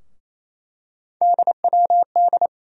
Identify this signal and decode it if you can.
Morse code, heard as a keyed tone, about 28 wpm, 700 Hz. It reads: DWD
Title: USA Call Sign Postfixes - 3 Letters 28wpm